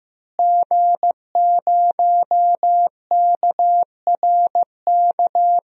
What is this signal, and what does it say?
Morse code, heard as a keyed tone, about 16 wpm, 700 Hz. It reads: G0KRK
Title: DXCC-100 Call Signs - 5 Characters - Repeated 3x - 15wpm